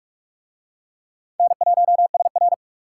Morse code, heard as a keyed tone, about 39 wpm, 700 Hz. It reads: N1SR